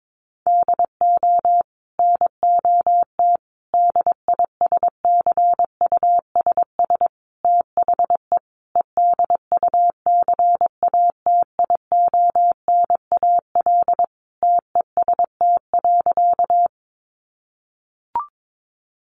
Morse code, heard as a keyed tone, about 22 wpm, 700 Hz. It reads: DO NOT DISCUSS THE EDUCATIONAL TEST.
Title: Sentences from Top 1200 Words 22wpm